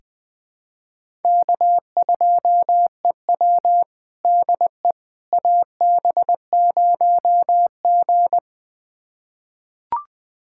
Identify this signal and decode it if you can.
Morse code, heard as a keyed tone, about 21 wpm, 700 Hz. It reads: K2EW DE AB0G
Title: USA Callsign DE Callsign - 4 Characters - 20wpm